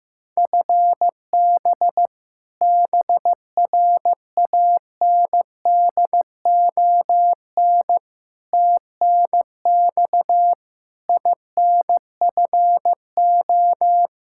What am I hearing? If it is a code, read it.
Morse code, heard as a keyed tone, about 15 wpm, 700 Hz. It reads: FB BRANDON TNX INFO
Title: Practice QSO - Exchange 3 - More Info - 15wpm